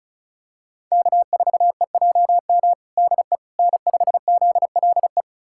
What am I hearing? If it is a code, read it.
Morse code, heard as a keyed tone, about 34 wpm, 700 Hz. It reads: K4EJM DE N5ZLE
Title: USA Callsign DE Callsign - 5 Characters - 35wpm